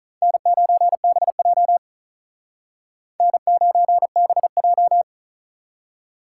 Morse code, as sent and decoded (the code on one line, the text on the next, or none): -. ----. -... .--- / -. ----. -... .---
N9BJ N9BJ